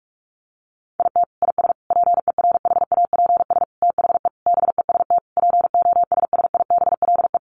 Morse code, heard as a keyed tone, about 44 wpm, 700 Hz. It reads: IT IS PERHAPS THE BEST POSSIBLE
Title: Sentences from Top 1300 Words 45wpm